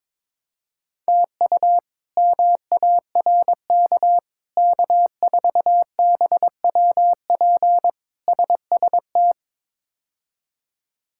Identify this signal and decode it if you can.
Morse code, heard as a keyed tone, about 22 wpm, 700 Hz. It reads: TU MARK K4BWP SST
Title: K1USN SST QSO Practice - 22wpm